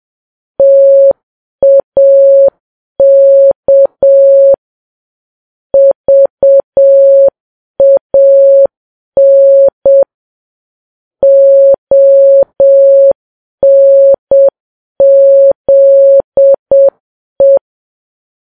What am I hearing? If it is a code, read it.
Morse code, heard as a keyed tone, about 7 wpm, 550 Hz. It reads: TAK VAN ONZE
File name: Files Web tls / Morse on9fv / morse7kps